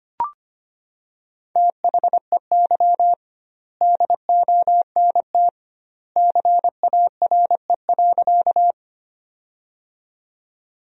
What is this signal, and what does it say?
Morse code, heard as a keyed tone, about 25 wpm, 700 Hz. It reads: THEY DONT CARE.
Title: 2x Word-Spacing - Sentences from Top 300 Words - 25wpm